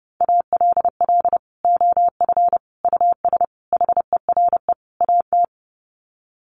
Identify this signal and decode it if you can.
Morse code, heard as a keyed tone, about 28 wpm, 700 Hz. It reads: ALL OF US HERE AT
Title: Sets of 5 Words 30wpm